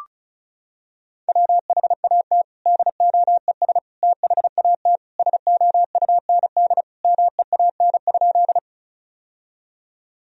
Morse code, heard as a keyed tone, about 36 wpm, 700 Hz. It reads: WHAT DOES THAT SOUND MEAN?
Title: Sentences from Top 1000 Words - Review Part 1 - 35wpm